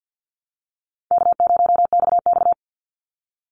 Morse code, heard as a keyed tone, about 50 wpm, 700 Hz. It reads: K0XX
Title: Easy Call Signs - 50wpm